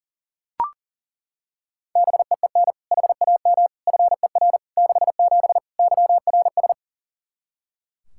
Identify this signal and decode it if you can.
Morse code, heard as a keyed tone, about 40 wpm, 700 Hz. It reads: BEEN HAM FER 67 YRS
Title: Practice QSO - Exchange 3 - More Info - 40wpm